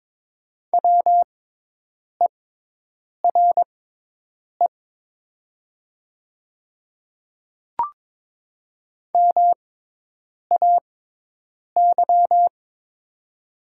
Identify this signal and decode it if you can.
Morse code, heard as a keyed tone, about 22 wpm, spaced as for 8 wpm, 700 Hz. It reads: WERE MAY
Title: 6x Character Spacing - Top 500 Words - 22wpm